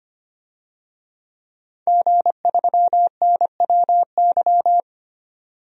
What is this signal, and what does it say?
Morse code, heard as a keyed tone, about 25 wpm, 700 Hz. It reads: G3NWY
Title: DXCC-100 Call Signs - 5 Characters - 25wpm